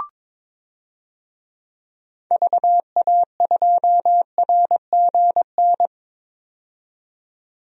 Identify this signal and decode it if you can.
Morse code, heard as a keyed tone, about 22 wpm, 700 Hz. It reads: VA2RGN
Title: DXCC-100 Call Signs - 6 Characters - 22wpm